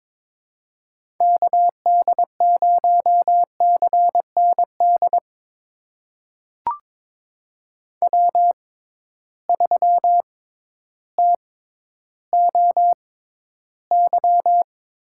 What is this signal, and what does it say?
Morse code, heard as a keyed tone, about 22 wpm, spaced as for 12 wpm, 700 Hz. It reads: KD0CND W3TOY
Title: USA Call Signs - 6x Character Spacing - 22wpm